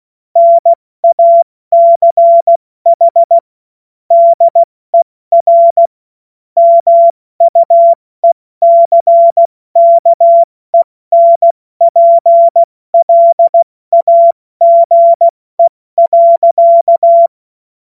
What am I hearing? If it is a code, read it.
Morse code, heard as a keyed tone, about 16 wpm, 680 Hz. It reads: NACH DER MUECKENPLAGE.
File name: Morsekurs / Witze / Witz 07. Februar 2023 70 F80